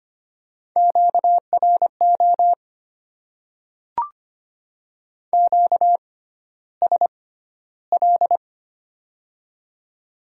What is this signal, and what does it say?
Morse code, heard as a keyed tone, about 25 wpm, spaced as for 13 wpm, 700 Hz. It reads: QRO QSL
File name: Top 90 QSO Elements - 6x Character Spacing - 25wpm